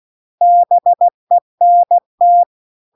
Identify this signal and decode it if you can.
Morse code, heard as a keyed tone, about 16 wpm, 700 Hz. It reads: BENT
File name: CW-The-Big-Twoheaded-River-2-16WPM